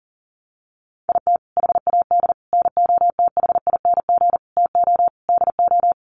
Morse code, heard as a keyed tone, about 40 wpm, 700 Hz. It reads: IT HAD NOTHING TO DO
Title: Sets of 5 Words Encore2 40wpm